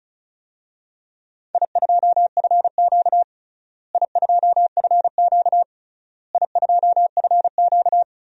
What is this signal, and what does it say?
Morse code, heard as a keyed tone, about 36 wpm, 700 Hz. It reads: I2FQ I2FQ I2FQ
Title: DXCC-100 Call Signs - 4 Characters - Repeated 3x - 35wpm